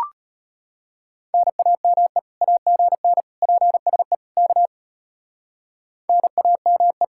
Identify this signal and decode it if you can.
Morse code, heard as a keyed tone, about 33 wpm, 700 Hz. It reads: NAME AGN PSE K NAME